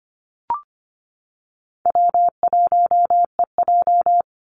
Morse code, heard as a keyed tone, about 25 wpm, 700 Hz. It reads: W1EJ
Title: NA SOTA Activator Callsigns 25wpm